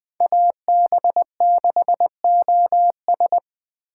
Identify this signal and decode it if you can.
Morse code, heard as a keyed tone, about 20 wpm, 700 Hz. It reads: AB6OS
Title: Speed Racing - USA Call Signs 17wpm